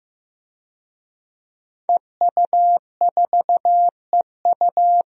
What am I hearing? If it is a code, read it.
Morse code, heard as a keyed tone, about 15 wpm, 700 Hz. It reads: EU4EU
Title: DXCC-100 Call Signs - 5 Characters - Repeated 2x - 15wpm